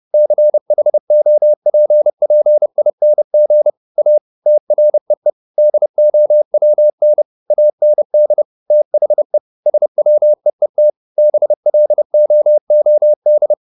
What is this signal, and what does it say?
Morse code, heard as a keyed tone, about 30 wpm, 600 Hz. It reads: CHOPPING A TREE DOWN AND THE SWEET BLOOD
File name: the old man and the sea / the old man and the sea 1 30wpm